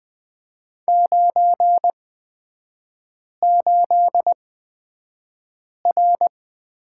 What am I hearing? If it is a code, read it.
Morse code, heard as a keyed tone, about 20 wpm, spaced as for 5 wpm, 700 Hz. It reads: 98R
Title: 250 Lesson 40e - BK - Three Characters 20wpm